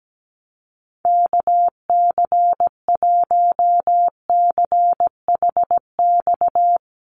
Morse code, heard as a keyed tone, about 17 wpm, 700 Hz. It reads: KC1CHX
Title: USA Call Signs Repeated 3x 17wpm